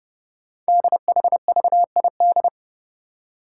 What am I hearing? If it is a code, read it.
Morse code, heard as a keyed tone, about 29 wpm, 690 Hz. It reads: DHVID
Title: Code Groups - 5 Letters - 30wpm